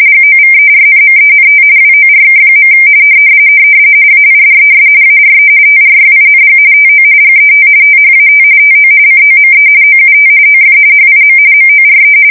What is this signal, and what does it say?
RTTY, about 45.45 baud. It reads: GOVERNMENT WORK. AIDES ARE TRYING TO GET THAT TO COME OUT OF HIM MORE
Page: 170 Shift 2125/2295